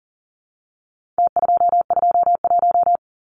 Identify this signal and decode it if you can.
Morse code, heard as a keyed tone, about 40 wpm, 700 Hz. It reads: T221
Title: Numbers - Contest Style - 4-Digit 40wpm